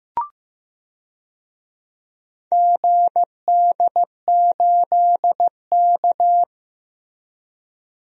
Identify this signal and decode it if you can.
Morse code, heard as a keyed tone, about 15 wpm, 700 Hz. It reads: GD8K
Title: DXCC-100 Call Signs - 4 Characters - Repeated 3x - 15wpm